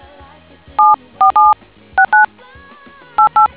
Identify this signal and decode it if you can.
DTMF, heard as telephone touch tones, tones of about 130 ms. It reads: 747 69 88